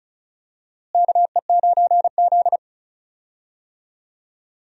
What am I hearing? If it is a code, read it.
Morse code, heard as a keyed tone, about 34 wpm, 700 Hz. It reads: KE9Z
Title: USA Call Signs - 4 Characters - Repeated 2x - 35wpm